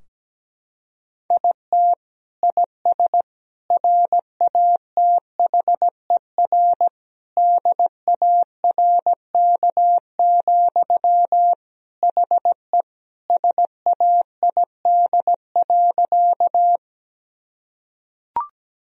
Morse code, heard as a keyed tone, about 17 wpm, 700 Hz. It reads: IT IS RATHER DARK, HE SAID.
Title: Sentences from Top 1000 Words - Review Part 4 - 17wpm